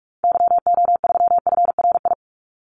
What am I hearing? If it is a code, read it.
Morse code, heard as a keyed tone, about 44 wpm, 700 Hz. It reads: YO3FRI
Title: Call Signs Repeated 2x - 45wpm